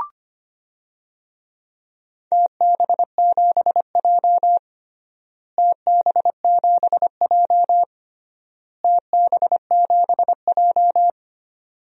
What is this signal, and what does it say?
Morse code, heard as a keyed tone, about 25 wpm, 700 Hz. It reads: TB7J TB7J TB7J
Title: DXCC-100 Call Signs - 4 Characters - Repeated 3x - 25wpm